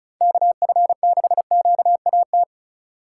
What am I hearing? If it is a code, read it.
Morse code, heard as a keyed tone, about 36 wpm, 700 Hz. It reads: KF6QAT
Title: Speed Racing - USA 2x3 Call Signs 30wpm